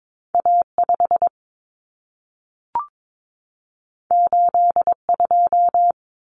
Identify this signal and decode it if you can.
Morse code, heard as a keyed tone, about 22 wpm, 700 Hz. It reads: A5 82
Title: Common Cut Numbers - 2-Digit 22wpm